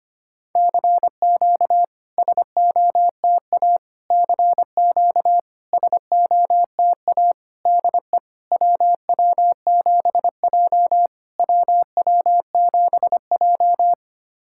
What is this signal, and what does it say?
Morse code, heard as a keyed tone, about 25 wpm, 690 Hz. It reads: CQ SOTA CQ SOTA DE WW7J WW7J
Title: SOTA QSO Practice 25wpm